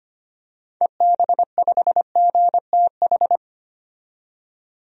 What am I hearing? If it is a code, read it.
Morse code, heard as a keyed tone, about 25 wpm, 700 Hz. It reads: EB5GTH
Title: DXCC-100 Call Signs - 6 Characters - 25wpm